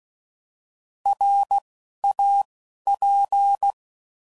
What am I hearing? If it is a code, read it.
Morse code, heard as a keyed tone, about 16 wpm, 800 Hz. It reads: R A P
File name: Prefixes / By First Letter / Rs